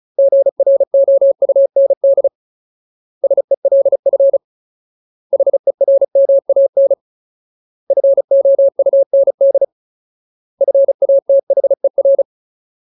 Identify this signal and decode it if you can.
Morse code, heard as a keyed tone, about 34 wpm, 550 Hz. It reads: GROUND SELF HERMAN FOUND FATHER
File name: Ultimate Word List / 35WPM Ultimate